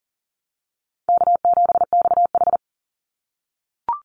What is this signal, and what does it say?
Morse code, heard as a keyed tone, about 40 wpm, 700 Hz. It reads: K7XH
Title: Easy Call Signs - 40wpm